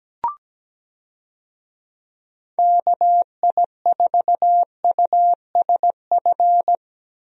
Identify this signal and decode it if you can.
Morse code, heard as a keyed tone, about 17 wpm, 700 Hz. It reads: KI4USF
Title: USA Call Signs - 6 Characters - Repeated 2x - 17wpm